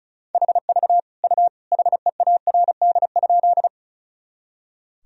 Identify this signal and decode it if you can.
Morse code, heard as a keyed tone, about 36 wpm, 700 Hz. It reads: HV U HEARD?
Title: Sentences from Top 400 Words and Abbreviations 35wpm